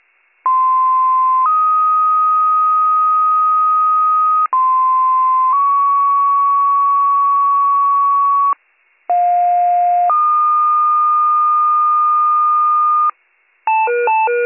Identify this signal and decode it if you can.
Two-tone page, heard as a one-second tone, then a three-second tone, about 1040 then 1250 Hz, about 1040 then 1120 Hz, about 710 then 1220 Hz. It is heard from 0.5 to 13.1 s.